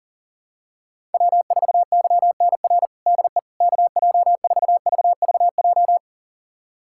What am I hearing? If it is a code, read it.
Morse code, heard as a keyed tone, about 40 wpm, 700 Hz. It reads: W4YNR DE KJ4VVJ
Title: USA Callsign DE Callsign 40wpm